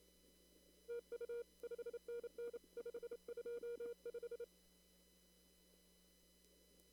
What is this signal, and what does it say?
Morse code, heard as a keyed tone, about 28 wpm, 470 Hz. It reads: TU 5NN 525
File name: CW / kt5j wpx cw-op-k5pi